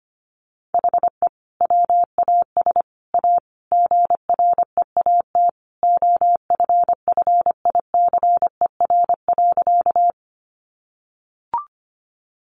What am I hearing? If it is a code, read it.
Morse code, heard as a keyed tone, about 25 wpm, 700 Hz. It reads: HE WAS A GREAT OFFICER.
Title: Sentences from Top 1200 Words 25wpm